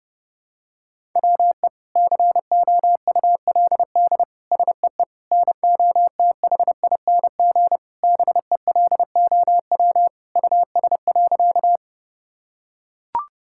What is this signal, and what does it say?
Morse code, heard as a keyed tone, about 32 wpm, 700 Hz. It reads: WE COULD SEE NOTHING BELOW US.
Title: Sentences from Top 1000 Words - Review Part 5 - 30wpm